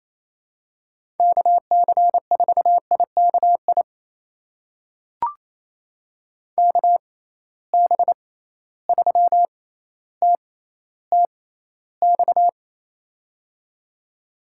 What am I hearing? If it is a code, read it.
Morse code, heard as a keyed tone, about 28 wpm, spaced as for 14 wpm, 700 Hz. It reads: KC4IKI KB3TTX
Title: USA Call Signs - 6 Characters - 6x Character Spacing - 28wpm